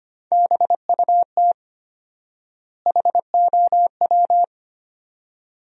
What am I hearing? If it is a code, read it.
Morse code, heard as a keyed tone, about 25 wpm, 700 Hz. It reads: BUT HOW
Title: 4x Word-Spacing - Sentences from Top 400 Words - 25wpm